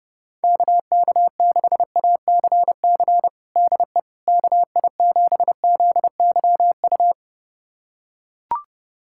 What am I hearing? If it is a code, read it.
Morse code, heard as a keyed tone, about 30 wpm, 690 Hz. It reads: KK6ACC DE KI7ZYU
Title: USA Callsign DE Callsign - 6 Characters - 30wpm